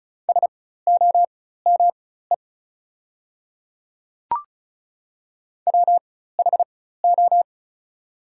Morse code, heard as a keyed tone, about 36 wpm, spaced as for 16 wpm, 700 Hz. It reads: SOME WHO
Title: 4x Character Spacing - Top 200 Words - 35wpm